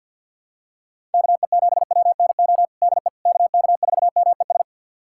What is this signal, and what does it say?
Morse code, heard as a keyed tone, about 53 wpm, 700 Hz. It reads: KE7WNO DE KK4MES
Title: USA Callsign DE Callsign - 6 Characters - 50wpm